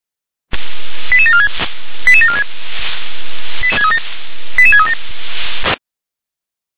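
Five-tone selective call (ZVEI), heard as quick runs of five tones, tones of about 70 ms. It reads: 88746 88746 88637
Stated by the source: ZVEI five-tone selective call received by SDR.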